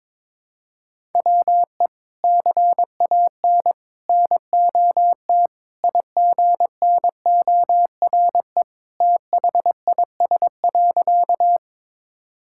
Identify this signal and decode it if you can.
Morse code, heard as a keyed tone, about 22 wpm, 700 Hz. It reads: WE CAN NOT IGNORE THIS.
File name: practice / 22wpm / Sentences with 3-Word Phrases - 22wpm